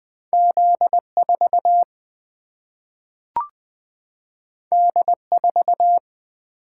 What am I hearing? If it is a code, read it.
Morse code, heard as a keyed tone, about 20 wpm, 700 Hz. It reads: Z 4 D 4
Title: Single-Letter Single-Number 20wpm